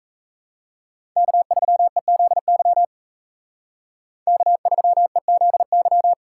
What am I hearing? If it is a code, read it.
Morse code, heard as a keyed tone, about 35 wpm, 700 Hz. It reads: K3EZY K3EZY